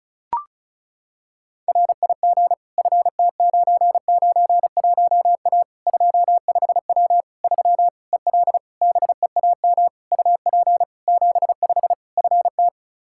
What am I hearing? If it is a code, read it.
Morse code, heard as a keyed tone, about 34 wpm, 700 Hz. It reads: RIG FT991A 25W 3 EL BEAM UP 75 FT
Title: K1USN SST QSO Practice 35wpm